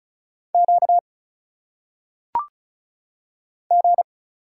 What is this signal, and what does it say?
Morse code, heard as a keyed tone, about 34 wpm, 700 Hz. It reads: Q G
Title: Single Letters - Phonetic Pronunciation 35wpm